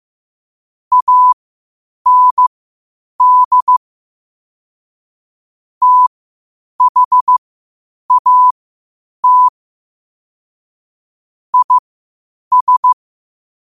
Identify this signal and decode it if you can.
Morse code, heard as a keyed tone, about 15 wpm, spaced as for 8 wpm, 1000 Hz. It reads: AND THAT IS
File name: qotdfiles / 8wpm